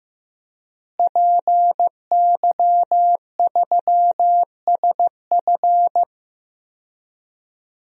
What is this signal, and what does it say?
Morse code, heard as a keyed tone, about 15 wpm, 700 Hz. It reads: PY3SF
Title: DXCC-100 Call Signs - 5 Characters - Repeated 2x - 15wpm